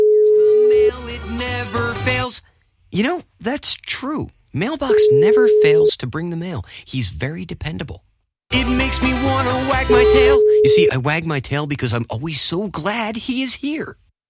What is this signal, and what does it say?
Ringback tone (Europe, 425 Hz), with speech.